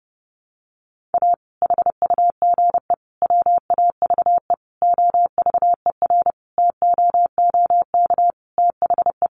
Morse code, heard as a keyed tone, about 30 wpm, 700 Hz. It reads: A HUGE WAVE OVERTOOK THE
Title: Sentences from Top 1400 Words 30wpm